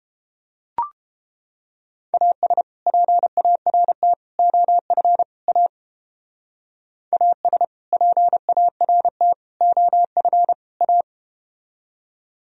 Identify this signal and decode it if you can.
Morse code, heard as a keyed tone, about 32 wpm, 700 Hz. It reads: AS PART OF A AS PART OF A